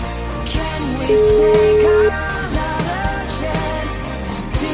Ringback tone (Europe, 425 Hz), with noise.